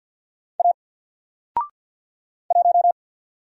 Morse code, heard as a keyed tone, about 46 wpm, 700 Hz. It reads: A 1
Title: Single Letter-Number 50wpm